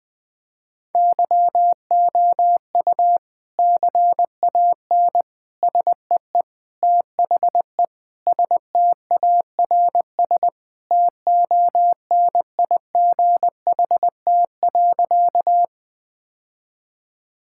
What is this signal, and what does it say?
Morse code, heard as a keyed tone, about 20 wpm, 700 Hz. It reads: YOU CAN SEE THE STARS TONIGHT.
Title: Sentences with 3-Word Phrases - 20wpm